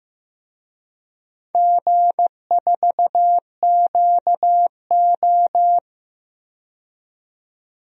Morse code, heard as a keyed tone, about 15 wpm, 700 Hz. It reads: G4QO
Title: DXCC-100 Call Signs - 4 Characters - 15wpm